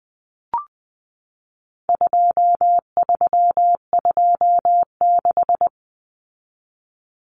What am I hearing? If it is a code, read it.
Morse code, heard as a keyed tone, about 20 wpm, 700 Hz. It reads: 2326
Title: Common Cut Numbers - 4-Digit 20wpm